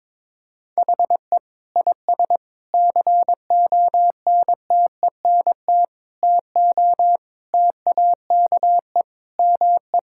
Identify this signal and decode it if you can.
Morse code, heard as a keyed tone, about 22 wpm, 700 Hz. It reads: HE IS CONTENT TO TAKE ME
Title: Sentences from Top 1000 Words - Review Part 6 - 22wpm